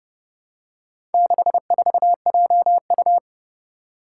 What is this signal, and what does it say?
Morse code, heard as a keyed tone, about 31 wpm, 700 Hz. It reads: 64JU
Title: Four Challenging Letter-Number Combinations 30wpm